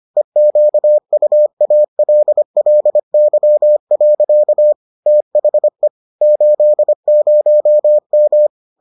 Morse code, heard as a keyed tone, about 25 wpm, 600 Hz. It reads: EQUALLY. THE 80M